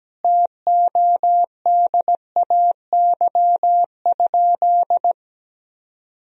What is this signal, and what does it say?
Morse code, heard as a keyed tone, about 17 wpm, 700 Hz. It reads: TODAY?
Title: Sentences from Top 1000 Words - Review Part 1 - 17wpm